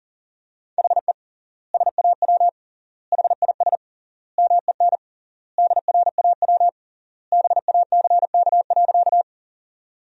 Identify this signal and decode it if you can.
Morse code, heard as a keyed tone, about 40 wpm, 700 Hz. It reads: HE SAW HIS MEN DRAW BACK.
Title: 3x Word-Spacing - Sentences from Top 300 Words - 40wpm